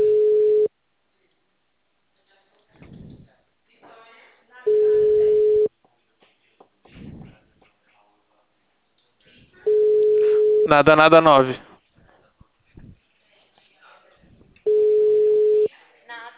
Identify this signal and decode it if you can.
Ringback tone (Europe, 425 Hz), with speech.
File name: out- -2011- - - .6